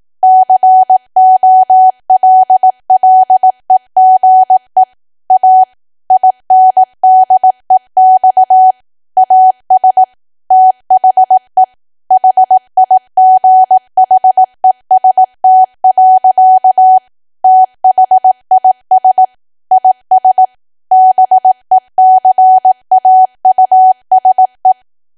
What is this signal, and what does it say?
Morse code, heard as a keyed tone, about 18 wpm, 750 Hz. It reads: COLLEGE A INDEX AS THE HIGHEST. THIS IS BECAUSE
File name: cwa / W1AW-Level2 / 18WPM